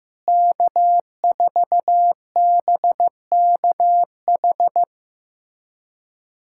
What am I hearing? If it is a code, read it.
Morse code, heard as a keyed tone, about 15 wpm, 700 Hz. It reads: K4BKH
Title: USA Callsign DE Callsign 15wpm